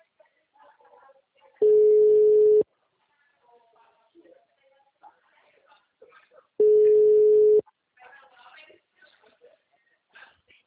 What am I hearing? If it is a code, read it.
Ringback tone (Europe, 425 Hz), with speech.